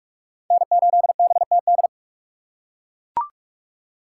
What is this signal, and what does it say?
Morse code, heard as a keyed tone, about 42 wpm, 700 Hz. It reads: N8BTD
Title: USA Call Signs - 5 Characters - 2x Character Spacing - 45wpm